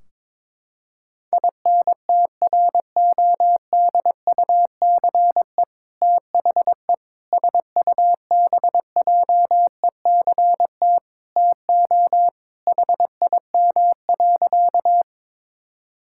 Morse code, heard as a keyed tone, about 22 wpm, 700 Hz. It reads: INTRODUCE THE SUBJECT TO HIM.
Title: Sentences from Top 1600 Words 22wpm